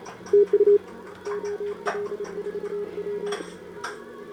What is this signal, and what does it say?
Morse code, heard as a keyed tone, about 29 wpm, 410 Hz. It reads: TU OL4W
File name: CQWWCW2025 / Sobota 10m CN3A